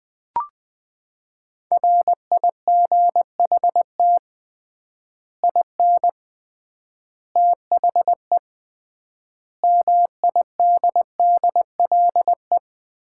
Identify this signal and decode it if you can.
Morse code, heard as a keyed tone, about 20 wpm, 700 Hz. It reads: RIGHT IN THE MIDDLE
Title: 3x Word-Spacing - Sets of 5 Words - 20wpm